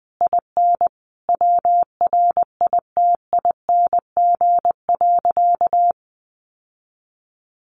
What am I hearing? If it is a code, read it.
Morse code, heard as a keyed tone, about 20 wpm, 700 Hz. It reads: IN WRITING.
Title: Sentences from Top 900 Words 20wpm